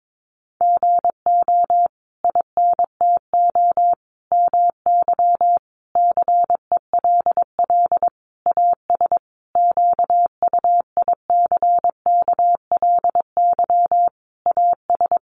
Morse code, heard as a keyed tone, about 22 wpm, 700 Hz. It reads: GO INTO MY CELL AS QUICKLY AS
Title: Sentences from Top 1100 Words 22wpm